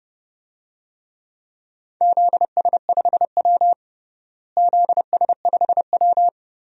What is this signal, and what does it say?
Morse code, heard as a keyed tone, about 30 wpm, 700 Hz. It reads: ZS5W ZS5W
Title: DXCC-100 Call Signs - 4 Characters - Repeated 2x - 30wpm